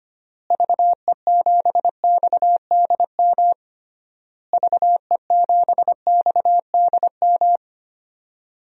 Morse code, heard as a keyed tone, about 25 wpm, 700 Hz. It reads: VE7XDM VE7XDM
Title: DXCC-100 Call Signs - 6 Characters - Repeated 3x - 25wpm